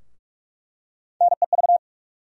Morse code, heard as a keyed tone, about 42 wpm, 700 Hz. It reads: NEV
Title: USA Call Sign Postfixes - 3 Letters 45wpm